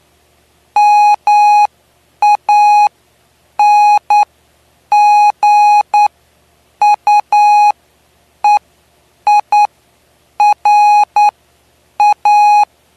Morse code, heard as a keyed tone, about 10 wpm, spaced as for 7 wpm, 830 Hz. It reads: MANGUEIRA
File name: cw / py2lcd / 09-PY2LCD-Li��o 9